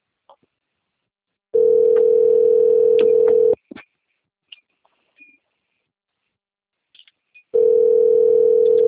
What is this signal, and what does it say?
Ringback tone (North America), 440 + 480 Hz.